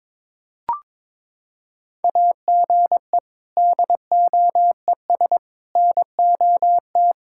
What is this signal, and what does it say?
Morse code, heard as a keyed tone, about 22 wpm, 700 Hz. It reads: AGE DOES NOT
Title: Sentences from Top 1000 Words - Review Part 6 - 22wpm